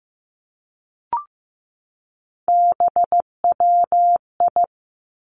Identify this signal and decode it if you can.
Morse code, heard as a keyed tone, about 15 wpm, 700 Hz. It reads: BWI
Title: Three Challenging Letter-Number Combinations 15wpm